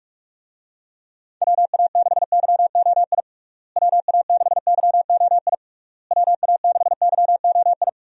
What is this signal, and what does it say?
Morse code, heard as a keyed tone, about 46 wpm, 700 Hz. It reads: WA6YOI WA6YOI WA6YOI
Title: USA Call Signs - 6 Characters - Repeated 3x - 45wpm